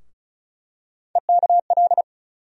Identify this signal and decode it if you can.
Morse code, heard as a keyed tone, about 34 wpm, 700 Hz. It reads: EKL
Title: USA Call Sign Postfixes - 3 Letters 35wpm